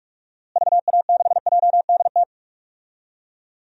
Morse code, heard as a keyed tone, about 43 wpm, 700 Hz. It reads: VA6JDT
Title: DXCC-100 Call Signs - 6 Characters - Repeated 3x - 45wpm